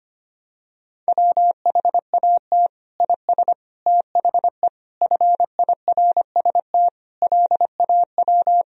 Morse code, heard as a keyed tone, about 25 wpm, 700 Hz. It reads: WHAT IS THE FIRST LAW
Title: Sentences from Top 1000 Words - Review Part 3 - 25wpm